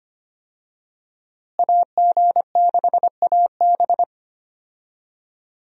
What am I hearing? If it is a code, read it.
Morse code, heard as a keyed tone, about 25 wpm, 700 Hz. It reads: AG6AB
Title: USA Call Signs - 5 Characters - Repeated 3x - 25wpm